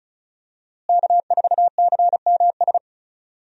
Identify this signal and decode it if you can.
Morse code, heard as a keyed tone, about 36 wpm, 700 Hz. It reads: K4CMS
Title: USA Call Signs - 5 Characters - Repeated 2x - 35wpm